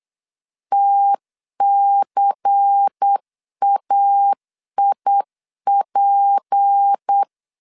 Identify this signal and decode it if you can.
Morse code, heard as a keyed tone, about 8 wpm, 800 Hz. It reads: TCAIP
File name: CW / DET / 1 9 1500C S A 8wpm 5637